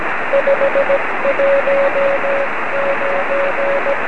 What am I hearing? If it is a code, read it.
Morse code, heard as a keyed tone, about 17 wpm, 550 Hz, with noise. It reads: 519
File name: 9cm